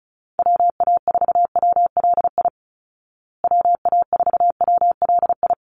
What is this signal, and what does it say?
Morse code, heard as a keyed tone, about 34 wpm, 700 Hz. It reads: WA4WLI WA4WLI
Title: Call Signs Repeated 2x - 35wpm